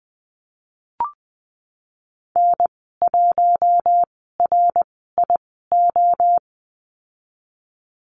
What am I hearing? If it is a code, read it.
Morse code, heard as a keyed tone, about 20 wpm, spaced as for 15 wpm, 700 Hz. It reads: N1RIO
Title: 082 Lesson 14g - Number 2 - Call Signs 20wpm